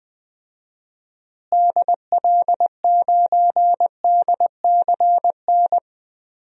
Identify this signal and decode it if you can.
Morse code, heard as a keyed tone, about 20 wpm, 700 Hz. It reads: DL9DCN
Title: DXCC-100 Call Signs - 6 Characters - Repeated 3x - 20wpm